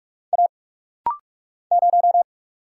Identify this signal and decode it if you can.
Morse code, heard as a keyed tone, about 46 wpm, 700 Hz. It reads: A 0
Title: Single Letter-Number Alt - Rapid-Fire 45wpm